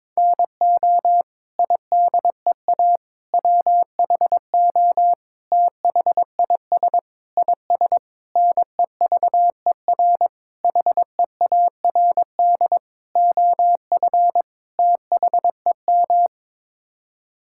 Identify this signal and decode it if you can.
Morse code, heard as a keyed tone, about 22 wpm, 700 Hz. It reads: NO IDEA WHO THIS IS NEVER HEARD OF THEM
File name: Sets of 9 Words - 22wpm